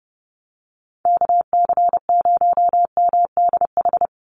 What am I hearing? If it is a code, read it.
Morse code, heard as a keyed tone, about 30 wpm, 700 Hz. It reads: KC0MDH
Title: USA Call Signs 30wpm